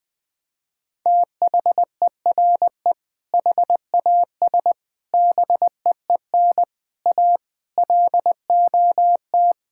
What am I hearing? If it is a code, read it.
Morse code, heard as a keyed tone, about 20 wpm, 700 Hz. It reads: THERE HAS BEEN A LOT
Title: Sets of 8 Words - 20wpm